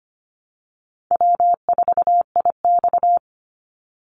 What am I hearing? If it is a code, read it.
Morse code, heard as a keyed tone, about 25 wpm, 700 Hz. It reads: W4IX
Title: Easy Call Signs Repeated 3x - 25wpm